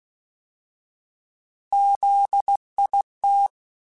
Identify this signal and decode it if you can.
Morse code, heard as a keyed tone, about 16 wpm, 800 Hz. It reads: ZIT
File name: Alphabetical / Alphabetical mp3s / Zs @15